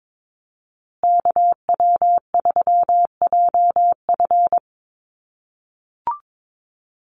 Morse code, heard as a keyed tone, about 22 wpm, 700 Hz. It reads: KW3JF
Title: USA Call Signs Repeated 3x 22wpm